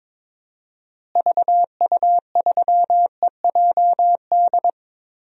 Morse code, heard as a keyed tone, about 22 wpm, 700 Hz. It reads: VU3EJD
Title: DXCC-100 Call Signs - 6 Characters - Repeated 3x - 22wpm